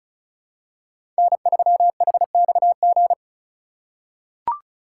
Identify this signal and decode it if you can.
Morse code, heard as a keyed tone, about 36 wpm, 700 Hz. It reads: N3HXG
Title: USA Call Signs - 5 Characters - 2x Character Spacing - 35wpm